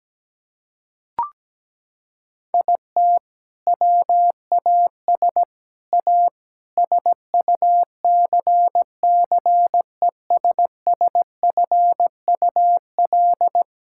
Morse code, heard as a keyed tone, about 17 wpm, 700 Hz. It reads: IT WAS A SUCCESSFUL
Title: Sentences from Top 1000 Words - Review Part 6 - 17wpm